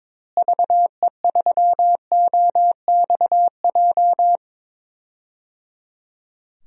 Morse code, heard as a keyed tone, about 22 wpm, 700 Hz. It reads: VE3OXJ
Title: DXCC-100 Call Signs - 6 Characters - Repeated 3x - 22wpm